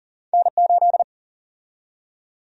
Morse code, heard as a keyed tone, about 40 wpm, 700 Hz. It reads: N8
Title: USA Call Sign Prefixes - 40wpm